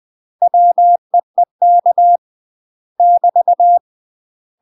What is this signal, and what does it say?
Morse code, heard as a keyed tone, about 20 wpm, 700 Hz. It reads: WEEK <BT>
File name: UK MASTER QSOs TRAINING / 20WPM / UK MASTER QSOs 0048